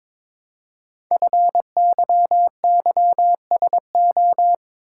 Morse code, heard as a keyed tone, about 22 wpm, 700 Hz. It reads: FYYSO
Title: Code Groups - 5 Letters - 22wpm